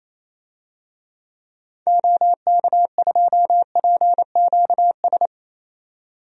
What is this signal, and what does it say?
Morse code, heard as a keyed tone, about 28 wpm, 700 Hz. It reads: OK2PQS
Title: DXCC-100 Call Signs - 6 Characters - 28wpm